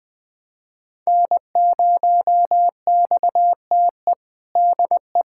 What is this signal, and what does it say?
Morse code, heard as a keyed tone, about 19 wpm, 700 Hz. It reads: N0XTE DE
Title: USA Callsign DE Callsign 20wpm